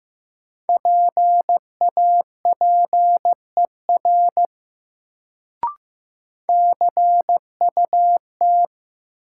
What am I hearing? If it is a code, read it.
Morse code, heard as a keyed tone, about 16 wpm, 700 Hz. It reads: PAPER CUT
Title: Sentences from Top 300 Words - New Word Drill - 15wpm